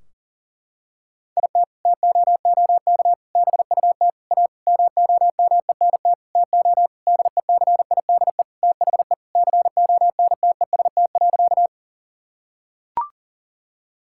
Morse code, heard as a keyed tone, about 40 wpm, 700 Hz. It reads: IT TOOK BUT A MOMENT TO DECIDE THE CONTEST.
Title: Sentences from Top 1600 Words 40wpm